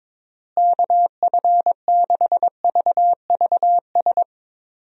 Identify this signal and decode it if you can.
Morse code, heard as a keyed tone, about 22 wpm, 700 Hz. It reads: KF6VVS
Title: USA Call Signs - 6 Characters - Repeated 3x - 22wpm